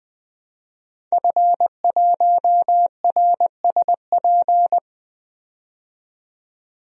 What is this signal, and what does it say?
Morse code, heard as a keyed tone, about 20 wpm, 700 Hz. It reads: F1RSP
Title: DXCC-100 Call Signs - 5 Characters - 20wpm